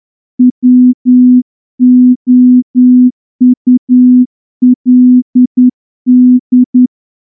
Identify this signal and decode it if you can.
Morse code, heard as a keyed tone, about 9 wpm, 250 Hz. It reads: WOULD
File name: morse-training-10WPM